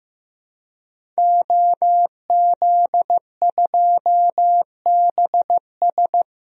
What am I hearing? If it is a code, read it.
Morse code, heard as a keyed tone, about 15 wpm, 700 Hz. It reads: OZ2BS
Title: DXCC-100 Call Signs - 5 Characters - Repeated 2x - 15wpm